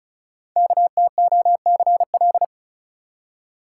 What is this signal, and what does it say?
Morse code, heard as a keyed tone, about 36 wpm, 700 Hz. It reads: KTOCL
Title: Code Groups - 5 Letters - 35wpm